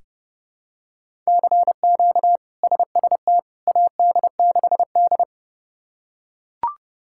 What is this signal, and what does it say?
Morse code, heard as a keyed tone, about 30 wpm, 700 Hz. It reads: CQ SST AD6D
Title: K1USN SST QSO Practice - 30wpm